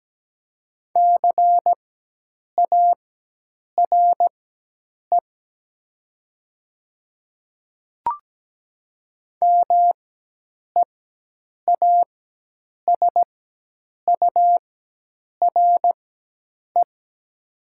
Morse code, heard as a keyed tone, about 17 wpm, spaced as for 8 wpm, 700 Hz. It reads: CARE MEASURE
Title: 4x Character Spacing - Top 500 Words - 17wpm